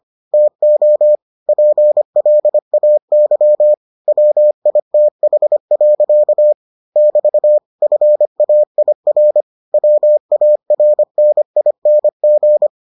Morse code, heard as a keyed tone, about 25 wpm, 600 Hz. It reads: TO PLAY WITH. <BT> FAIR WARNING